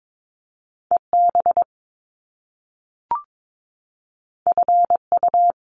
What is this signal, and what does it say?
Morse code, heard as a keyed tone, about 22 wpm, 700 Hz. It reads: EB FU
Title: Top Two Letter Combinations in Words 22wpm